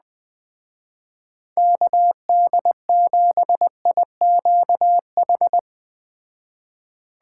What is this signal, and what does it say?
Morse code, heard as a keyed tone, about 19 wpm, 700 Hz. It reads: KD7IQH
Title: Speed Racing - USA 2x3 Call Signs 17wpm